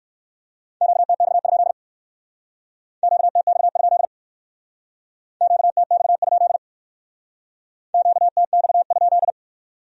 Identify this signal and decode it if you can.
Morse code, heard as a keyed tone, about 58 wpm, 700 Hz. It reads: QTX? QTX? QTX? QTX?